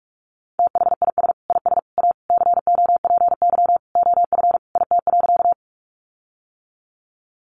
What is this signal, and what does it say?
Morse code, heard as a keyed tone, about 44 wpm, 700 Hz. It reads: THIS IS A COPY OF IT.
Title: Sentences from Top 1400 Words 45wpm